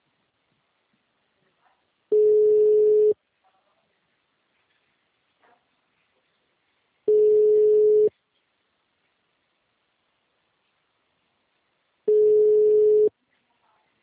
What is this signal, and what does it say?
Ringback tone (Europe, 425 Hz).
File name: out- -2014- - - .56